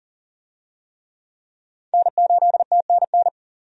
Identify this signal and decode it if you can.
Morse code, heard as a keyed tone, about 37 wpm, 700 Hz. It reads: N8TNN
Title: USA Call Signs - 5 Characters - 40wpm